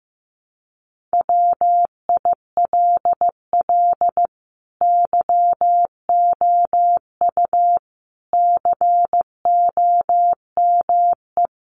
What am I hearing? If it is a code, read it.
Morse code, heard as a keyed tone, about 15 wpm, 700 Hz. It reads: WILL YOU COME
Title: Sentences from Top 1500 Words 15wpm